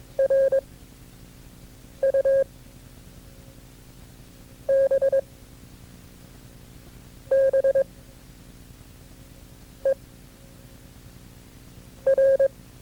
Morse code, heard as a keyed tone, about 22 wpm, 540 Hz. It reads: R U B B E R
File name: FreeFiles / CW-practice / KB6MT-tape2-side1